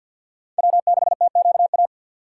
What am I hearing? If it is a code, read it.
Morse code, heard as a keyed tone, about 52 wpm, 700 Hz. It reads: W6TQA
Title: USA Call Signs - 5 Characters - 4x Character Spacing - 50wpm